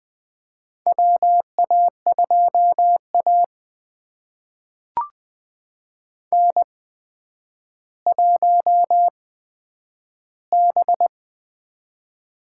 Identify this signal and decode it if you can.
Morse code, heard as a keyed tone, about 20 wpm, spaced as for 10 wpm, 700 Hz. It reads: WA2A N1B
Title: USA Call Signs - 4 Characters - 8x Character Spacing - 20wpm